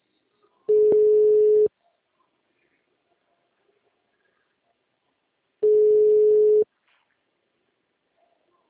Ringback tone (Europe, 425 Hz).